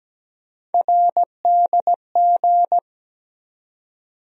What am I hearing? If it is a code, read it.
Morse code, heard as a keyed tone, about 17 wpm, 700 Hz. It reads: RDG
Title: USA Call Sign Postfixes - 3 Letters 17wpm